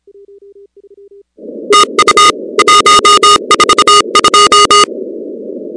Morse code, heard as a keyed tone, about 30 wpm, 1230 Hz, with noise. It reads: TU 142
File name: contests / 2019-cw-wpx-kt5j-k5pi- -2330 0-16